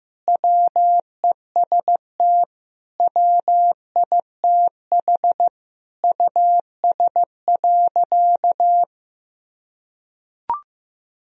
Morse code, heard as a keyed tone, about 16 wpm, 700 Hz. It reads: WEST WITH US.
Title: Sentences from Top 1000 Words - Review Part 2 - 15wpm